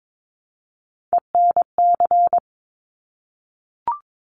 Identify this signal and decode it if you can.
Morse code, heard as a keyed tone, about 22 wpm, 700 Hz. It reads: ENC
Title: Top Three Letter Combinations in Words 22wpm